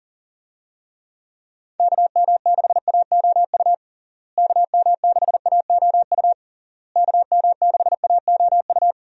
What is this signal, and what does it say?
Morse code, heard as a keyed tone, about 40 wpm, 700 Hz. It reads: KM6AOU KM6AOU KM6AOU
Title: USA Call Signs - 6 Characters - Repeated 3x - 40wpm